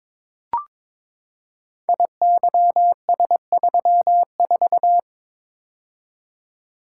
Morse code, heard as a keyed tone, about 22 wpm, 700 Hz. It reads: IYS34
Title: Code Groups - 5 Letter:Numbers - 22wpm